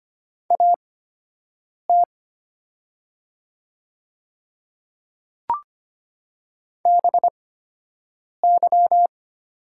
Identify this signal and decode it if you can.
Morse code, heard as a keyed tone, about 25 wpm, spaced as for 7 wpm, 690 Hz. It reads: AT BY
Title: 8x Character Spacing - Top 300 Words - 25wpm